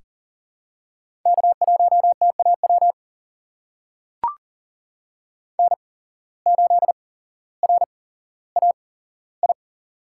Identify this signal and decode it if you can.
Morse code, heard as a keyed tone, about 42 wpm, spaced as for 17 wpm, 700 Hz. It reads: K1TAW N8RAI
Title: USA Call Signs - 5 Characters - 8x Character Spacing - 40wpm